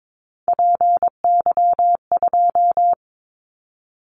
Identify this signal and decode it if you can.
Morse code, heard as a keyed tone, about 22 wpm, 700 Hz. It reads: PY2
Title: Callsign Prefixes for Top 100 DXCC - 22wpm